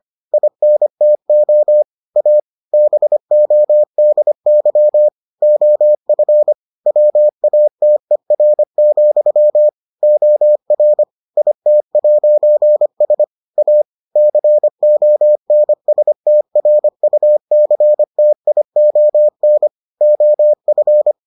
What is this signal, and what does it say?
Morse code, heard as a keyed tone, about 25 wpm, 600 Hz. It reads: INTO A BODY OF WATER, OR IT'S A CONSTRUCTION OF